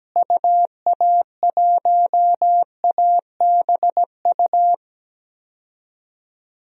Morse code, heard as a keyed tone, about 17 wpm, 700 Hz. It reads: UA1ABU
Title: DXCC-100 Call Signs - 6 Characters - Repeated 2x - 17wpm